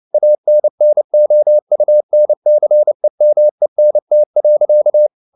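Morse code, heard as a keyed tone, about 29 wpm, 600 Hz. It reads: ANNOUNCEMENT.
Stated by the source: Morse code at 29 wpm, sending Ken Follett's novel Never in English, from CW med Gnister.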